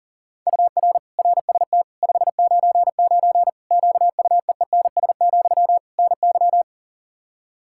Morse code, heard as a keyed tone, about 37 wpm, 700 Hz. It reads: UR RST 599 QUEENS, NY
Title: Practice QSO - Exchange 1 - Beginning - 40wpm